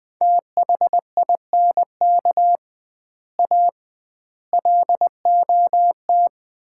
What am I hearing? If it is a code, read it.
Morse code, heard as a keyed tone, about 20 wpm, 700 Hz. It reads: THINK A LOT
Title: 2x Word-Spacing - Sets of 5 Words - 20wpm